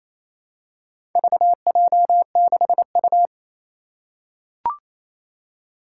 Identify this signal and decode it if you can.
Morse code, heard as a keyed tone, about 29 wpm, 700 Hz. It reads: VJ6U
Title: Four Challenging Letter-Number Combinations 28wpm